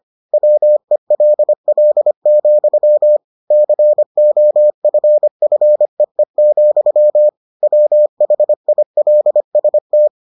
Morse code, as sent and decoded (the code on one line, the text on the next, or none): .-- . .-.. .-.. --..-- / -.-. --- ..-. ..-. . --..-- / .-- .... .. .-.. ... -
WELL, COFFEE, WHILST